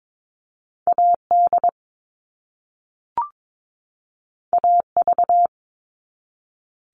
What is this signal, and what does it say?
Morse code, heard as a keyed tone, about 22 wpm, 700 Hz. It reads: AD AV
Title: All Cut Numbers - 2-Digit 22wpm